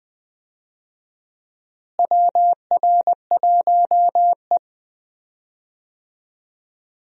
Morse code, heard as a keyed tone, about 20 wpm, 700 Hz. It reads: WR1E
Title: USA Call Signs - 4 Characters - Repeated 3x - 20wpm